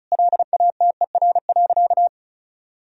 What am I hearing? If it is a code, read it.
Morse code, heard as a keyed tone, about 36 wpm, 700 Hz. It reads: LATER.
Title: Sentences from Top 1000 Words - Review Part 4 - 35wpm